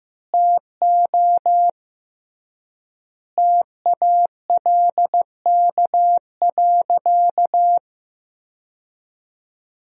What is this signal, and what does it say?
Morse code, heard as a keyed tone, about 15 wpm, 700 Hz. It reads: TO TALK.
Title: 3x Word-Spacing - Sentences from Top 400 Words - 15wpm